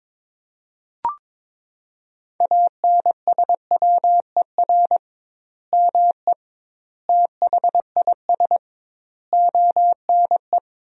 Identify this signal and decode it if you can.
Morse code, heard as a keyed tone, about 22 wpm, 700 Hz. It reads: ANSWER ME THIS ONE
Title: 2x Word-Spacing - Sentences from Top 400 Words - 22wpm